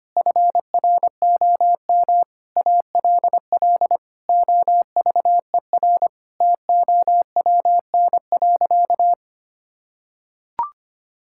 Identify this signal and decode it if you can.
Morse code, heard as a keyed tone, about 25 wpm, 700 Hz. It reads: FROM ALL OVER TOWN.
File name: practice / 25wpm / Difficult Sentences with Hints - 25wpm